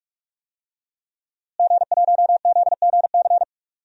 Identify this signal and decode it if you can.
Morse code, heard as a keyed tone, about 44 wpm, 700 Hz. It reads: G1ZGC
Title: DXCC-100 Call Signs - 5 Characters - Repeated 3x - 45wpm